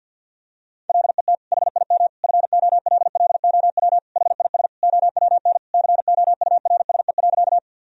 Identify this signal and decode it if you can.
Morse code, heard as a keyed tone, about 52 wpm, 700 Hz. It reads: LET HIM FOLLOW HIS OWN COURSE.
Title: Sentences from Top 1000 Words - Review Part 1 - 50wpm